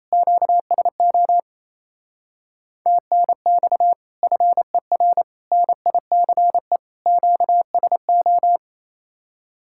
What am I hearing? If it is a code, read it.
Morse code, heard as a keyed tone, about 33 wpm, 700 Hz. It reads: QSO TNX FER NICE QSO
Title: Speed-Racing Common QSO Phrases Encore - 28wpm